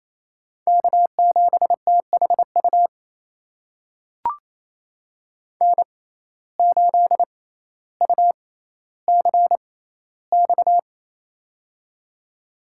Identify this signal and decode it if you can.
Morse code, heard as a keyed tone, about 28 wpm, spaced as for 14 wpm, 700 Hz. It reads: K7THU N8UCX
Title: USA Call Signs - 5 Characters - 6x Character Spacing - 28wpm